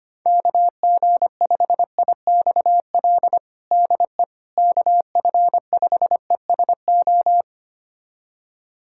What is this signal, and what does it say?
Morse code, heard as a keyed tone, about 25 wpm, 700 Hz. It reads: KG5IXL DE KF5ESO
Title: USA Callsign DE Callsign - 6 Characters - 25wpm